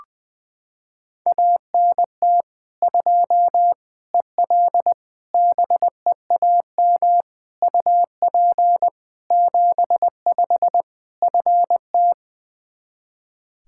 Morse code, heard as a keyed tone, about 20 wpm, 700 Hz. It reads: ANT 2 EL BEAM UP 75 FT
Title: K1USN SST QSO Practice 20wpm